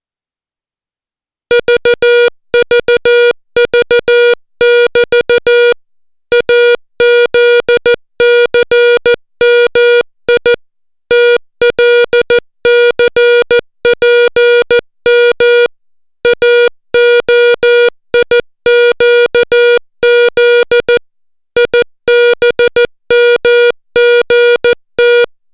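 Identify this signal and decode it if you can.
Morse code, heard as a keyed tone, about 14 wpm, 480 Hz. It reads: VVV<BT> AZCMI TLCPM AOIQZ IBMGT
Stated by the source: Morse alphabet lesson, 60 characters per minute.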